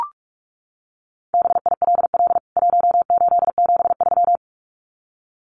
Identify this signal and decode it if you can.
Morse code, heard as a keyed tone, about 44 wpm, 700 Hz. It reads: BILL 1873
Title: CWT Exchange 45wpm